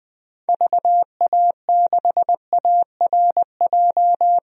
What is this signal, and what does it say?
Morse code, heard as a keyed tone, about 20 wpm, 700 Hz. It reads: VA6ARJ
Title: DXCC-100 Call Signs - 6 Characters - Repeated 2x - 20wpm